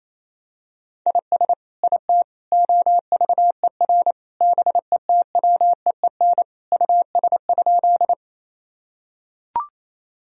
Morse code, heard as a keyed tone, about 28 wpm, 700 Hz. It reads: IS IT OVER BETWEEN US?
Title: Sentences from Top 1000 Words - Review Part 1 - 28wpm